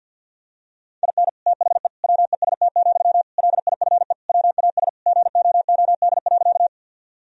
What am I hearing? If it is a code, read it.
Morse code, heard as a keyed tone, about 50 wpm, 700 Hz. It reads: IN THE WEST, LIFE WAS GOOD.
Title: Sentences from Top 1000 Words - Review Part 2 - 50wpm